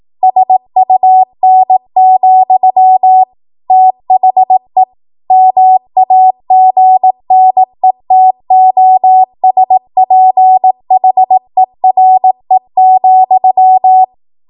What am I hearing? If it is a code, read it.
Morse code, heard as a keyed tone, about 18 wpm, 750 Hz. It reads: SUN, THE MAGNETOSPHERE,
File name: Archive / 18 WPM / 18WPM